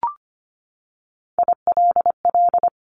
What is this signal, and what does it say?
Morse code, heard as a keyed tone, about 25 wpm, 700 Hz. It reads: ILL
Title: Top 1751-2000 Words 25wpm